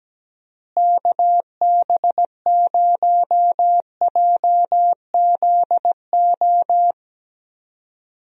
Morse code, heard as a keyed tone, about 17 wpm, 700 Hz. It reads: KB0JZO
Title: USA Call Signs - 6 Characters - Repeated 2x - 17wpm